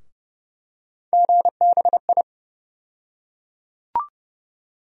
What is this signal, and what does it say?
Morse code, heard as a keyed tone, about 32 wpm, 700 Hz. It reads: GBI
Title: USA Call Sign Postfixes - 3 Letters 30wpm